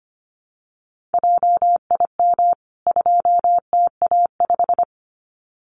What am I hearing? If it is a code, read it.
Morse code, heard as a keyed tone, about 25 wpm, 700 Hz. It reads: JIM 2TA5
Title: CWT Exchange 25wpm